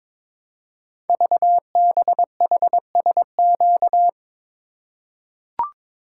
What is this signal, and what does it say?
Morse code, heard as a keyed tone, about 22 wpm, 700 Hz. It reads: VBHSQ
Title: Code Groups - 5 Letter:Numbers - 22wpm